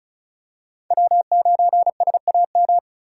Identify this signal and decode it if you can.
Morse code, heard as a keyed tone, about 36 wpm, 700 Hz. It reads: W9SAM
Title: USA Call Signs - 5 Characters - 2x Character Spacing - 35wpm